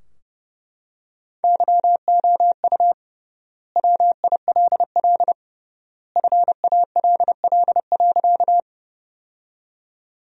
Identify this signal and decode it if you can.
Morse code, heard as a keyed tone, about 28 wpm, 700 Hz. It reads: YOU WILL FALL.
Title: 3x Word-Spacing - Sentences from Top 500 Words - 30wpm